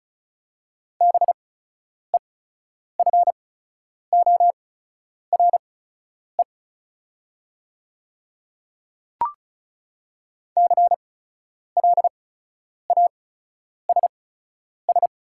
Morse code, heard as a keyed tone, about 36 wpm, spaced as for 10 wpm, 690 Hz. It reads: BEFORE CLASS
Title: 8x Character Spacing - Top 500 Words - 35wpm